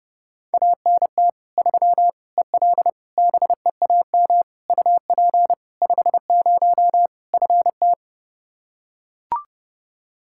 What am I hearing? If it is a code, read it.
Morse code, heard as a keyed tone, about 30 wpm, 700 Hz. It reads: ANT 3 EL BEAM UP 50 FT
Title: Practice QSO - Exchange 2 - Info - 30wpm